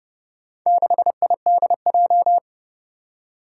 Morse code, heard as a keyed tone, about 30 wpm, 700 Hz. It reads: 6IDJ
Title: Four Challenging Letter-Number Combinations 30wpm